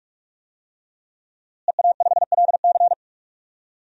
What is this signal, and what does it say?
Morse code, heard as a keyed tone, about 46 wpm, 700 Hz. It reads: EA5LC
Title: DXCC-100 Call Signs - 5 Characters - Repeated 2x - 45wpm